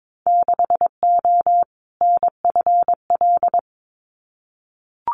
Morse code, heard as a keyed tone, about 22 wpm, 700 Hz. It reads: 6O NFL
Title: Winter Field Day Exchanges 22wpm